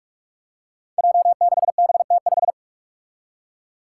Morse code, heard as a keyed tone, about 44 wpm, 700 Hz. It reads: J6BT5
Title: Five Challenging Letter-Number Combinations 45wpm